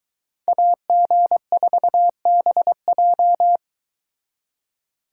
Morse code, heard as a keyed tone, about 22 wpm, 700 Hz. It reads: AG4BJ